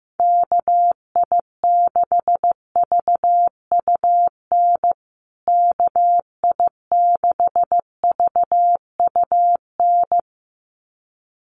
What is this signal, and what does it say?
Morse code, heard as a keyed tone, about 15 wpm, 700 Hz. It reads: KI6VUN KI6VUN
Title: USA Call Signs Repeated 2x 15wpm